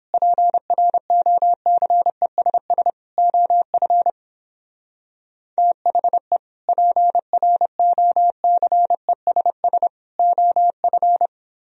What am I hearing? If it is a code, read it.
Morse code, heard as a keyed tone, about 31 wpm, 700 Hz. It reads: PROCESS OF THE PROCESS OF